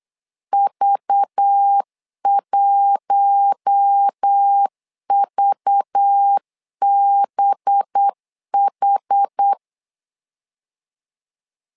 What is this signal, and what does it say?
Morse code, heard as a keyed tone, about 8 wpm, 800 Hz. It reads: V1VBH
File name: LEZIONI / 5 LBVC / 5 LBV 8wpm 2837